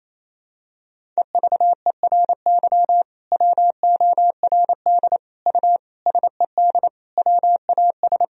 Morse code, heard as a keyed tone, about 28 wpm, 700 Hz. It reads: EVERY WORD U SED WAS
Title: Sentences from Top 400 Words and Abbreviations 28wpm